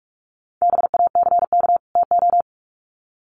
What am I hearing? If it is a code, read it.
Morse code, heard as a keyed tone, about 44 wpm, 700 Hz. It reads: BACK TO
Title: Sets of 2 Words 45wpm